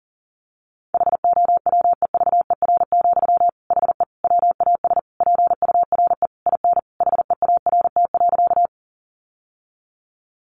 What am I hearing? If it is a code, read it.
Morse code, heard as a keyed tone, about 40 wpm, 700 Hz. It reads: HOWEVER, HE WAS PURE IN HEART.
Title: Sentences from Top 1500 Words 40wpm